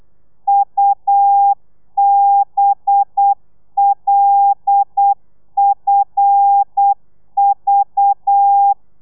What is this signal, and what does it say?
Morse code, heard as a keyed tone, about 8 wpm, 800 Hz. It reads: UBLFV
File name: telegrafia / i2tfj / Lez5 35